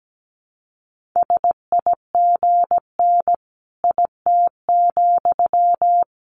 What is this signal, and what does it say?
Morse code, heard as a keyed tone, about 17 wpm, 700 Hz. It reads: SIGN IT,
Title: Sentences from Top 1000 Words 17wpm